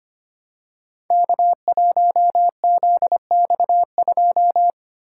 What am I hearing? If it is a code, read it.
Morse code, heard as a keyed tone, about 25 wpm, 700 Hz. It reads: K1ZX2
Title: Code Groups - 5 Letter:Numbers - 25wpm